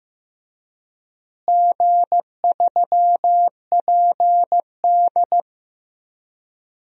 Morse code, heard as a keyed tone, about 16 wpm, 700 Hz. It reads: G3PD
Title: DXCC-100 Call Signs - 4 Characters - Repeated 2x - 15wpm